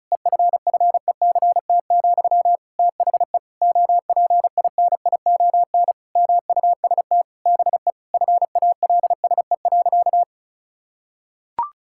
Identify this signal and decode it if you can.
Morse code, heard as a keyed tone, about 36 wpm, 700 Hz. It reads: EFFECT, THE OPINION MUST BE FALSE.
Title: Sentences from Top 1600 Words 35wpm